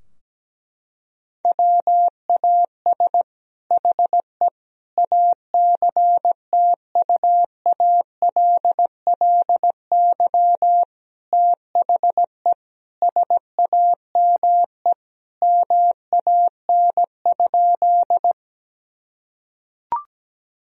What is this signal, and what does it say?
Morse code, heard as a keyed tone, about 17 wpm, 700 Hz. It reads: WAS HE ACTUALLY THE SAME MAN?
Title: Sentences from Top 1000 Words - Review Part 4 - 17wpm